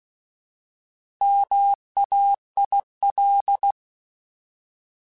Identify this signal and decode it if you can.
Morse code, heard as a keyed tone, about 16 wpm, 800 Hz. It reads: MAIL
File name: Alphabetical / Alphabetical mp3s / Ms @15